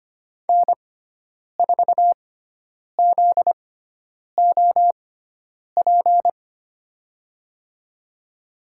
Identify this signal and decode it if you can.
Morse code, heard as a keyed tone, about 25 wpm, spaced as for 9 wpm, 700 Hz. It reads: N4ZOP
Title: USA Call Signs - 6x Character Spacing - 25wpm